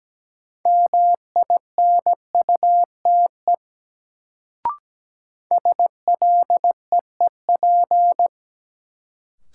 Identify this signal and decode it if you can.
Morse code, heard as a keyed tone, about 17 wpm, 700 Hz. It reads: MINUTE SLEEP
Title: Sentences from Top 500 Words - New Word Drill - 17wpm